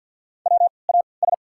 Morse code, heard as a keyed tone, about 53 wpm, spaced as for 29 wpm, 700 Hz. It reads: WAS
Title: 3x Character Spacing - Top 200 Words - 50wpm